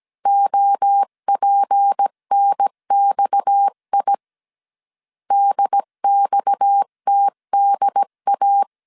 Morse code, heard as a keyed tone, about 17 wpm, 800 Hz. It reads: OPNXI DXTDA
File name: CW / DET / 1 6 1500C S C 16wpm 2733